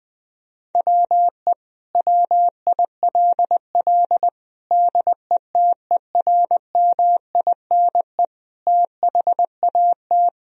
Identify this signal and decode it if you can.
Morse code, heard as a keyed tone, about 20 wpm, 700 Hz. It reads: WE WILL DETERMINE THAT
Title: Sentences from Top 1000 Words - Review Part 4 - 20wpm